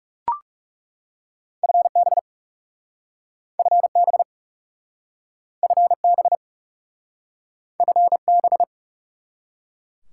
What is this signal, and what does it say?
Morse code, heard as a keyed tone, about 40 wpm, 700 Hz. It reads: FB FB FB FB